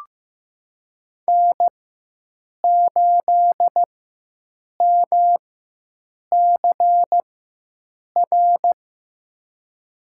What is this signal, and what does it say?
Morse code, heard as a keyed tone, about 15 wpm, spaced as for 7 wpm, 700 Hz. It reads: N8MCR
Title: USA Call Signs - 4x Character Spacing - 15wpm